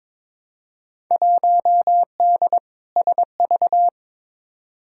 Morse code, heard as a keyed tone, about 23 wpm, 700 Hz. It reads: 1D SV
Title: Field Day Exchanges - 22wpm